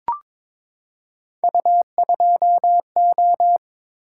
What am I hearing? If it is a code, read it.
Morse code, heard as a keyed tone, about 22 wpm, 700 Hz. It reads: U2O
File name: practice / 22wpm / Three Letter-Number Combinations - 22wpm